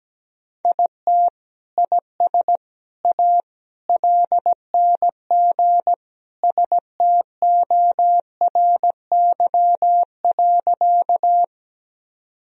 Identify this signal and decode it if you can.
Morse code, heard as a keyed tone, about 17 wpm, 700 Hz. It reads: IT IS A LNG STORY.
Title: Sentences from Top 300 Words and Abbreviations 17wpm